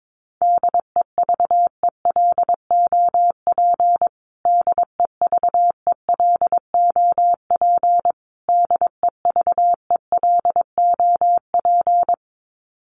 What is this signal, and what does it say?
Morse code, heard as a keyed tone, about 22 wpm, 700 Hz. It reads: DEVELOP DEVELOP DEVELOP
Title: Top 500 Words Repeated 22wpm